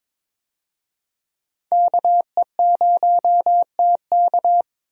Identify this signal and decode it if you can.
Morse code, heard as a keyed tone, about 21 wpm, 700 Hz. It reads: KE0TK
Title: USA Call Signs - 5 Characters - 22wpm